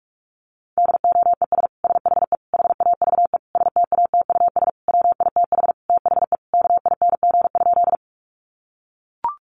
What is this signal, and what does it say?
Morse code, heard as a keyed tone, about 44 wpm, 700 Hz. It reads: DOES SHE HAVE STATUS WITH THE KING?
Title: Sentences from Top 1400 Words 45wpm